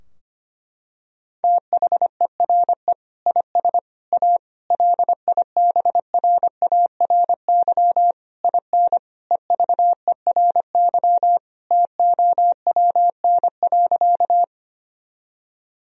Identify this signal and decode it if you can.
Morse code, heard as a keyed tone, about 25 wpm, 700 Hz. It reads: THERE IS A LIBRARY IN EVERY TOWN.
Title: Sentences from Top 1000 Words - Review Part 4 - 25wpm